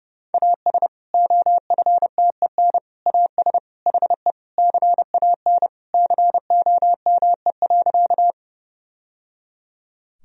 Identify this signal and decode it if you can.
Morse code, heard as a keyed tone, about 30 wpm, 700 Hz. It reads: AS OFTEN AS HE CAN COME.
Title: Sentences from Top 1000 Words - Review Part 1 - 30wpm